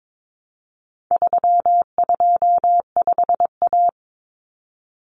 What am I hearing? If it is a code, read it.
Morse code, heard as a keyed tone, about 22 wpm, 700 Hz. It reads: 325A
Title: Common Cut Numbers - 4-Digit 22wpm